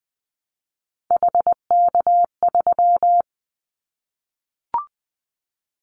Morse code, heard as a keyed tone, about 20 wpm, 700 Hz. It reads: HK3
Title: Callsign Prefixes for Top 100 DXCC - 20wpm